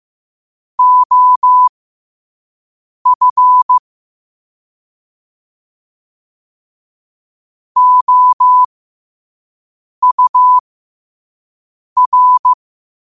Morse code, heard as a keyed tone, about 15 wpm, spaced as for 5 wpm, 1000 Hz. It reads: OF OUR
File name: qotdfiles / 5wpm